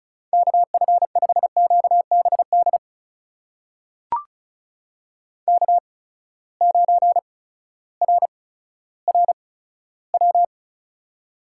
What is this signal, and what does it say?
Morse code, heard as a keyed tone, about 36 wpm, spaced as for 16 wpm, 700 Hz. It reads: KF5QBD K9RRW
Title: USA Call Signs - 8x Character Spacing - 35wpm